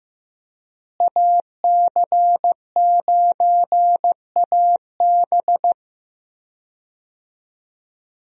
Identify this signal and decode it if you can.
Morse code, heard as a keyed tone, about 15 wpm, 700 Hz. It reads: AC9AB
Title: USA Call Signs - 5 Characters - Repeated 2x - 15wpm